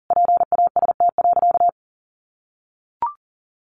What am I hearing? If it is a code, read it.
Morse code, heard as a keyed tone, about 40 wpm, 700 Hz. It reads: PAST.
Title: Sentences from Top 1200 Words 40wpm